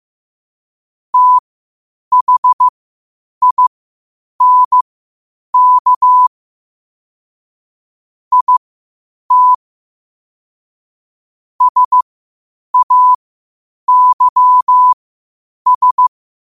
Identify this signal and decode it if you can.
Morse code, heard as a keyed tone, about 15 wpm, spaced as for 8 wpm, 1000 Hz. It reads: THINK IT SAYS
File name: qotdfiles / 8wpm